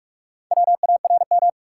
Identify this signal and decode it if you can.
Morse code, heard as a keyed tone, about 46 wpm, 700 Hz. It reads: WARM
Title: Four Letter Words - 45wpm